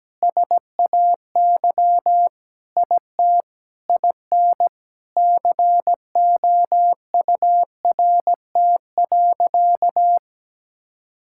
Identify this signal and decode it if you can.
Morse code, heard as a keyed tone, about 17 wpm, 700 Hz. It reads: SAY IT IN COURT.
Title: Sentences from Top 1600 Words 17wpm